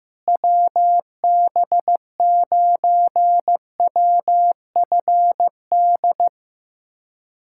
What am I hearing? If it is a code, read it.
Morse code, heard as a keyed tone, about 15 wpm, 700 Hz. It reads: WB9WFD
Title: USA Callsign DE Callsign 15wpm